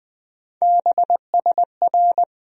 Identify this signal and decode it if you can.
Morse code, heard as a keyed tone, about 20 wpm, 700 Hz. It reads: BSR
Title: USA Call Sign Postfixes - 3 Letters 20wpm